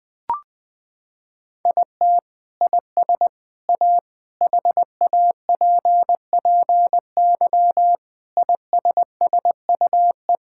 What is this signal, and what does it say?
Morse code, heard as a keyed tone, about 21 wpm, 700 Hz. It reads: IT IS A HAPPY ISSUE
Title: Sentences from Top 1000 Words - Review Part 6 - 20wpm